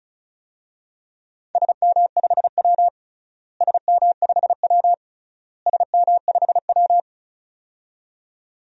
Morse code, heard as a keyed tone, about 34 wpm, 700 Hz. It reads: SM5W SM5W SM5W
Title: DXCC-100 Call Signs - 4 Characters - Repeated 3x - 35wpm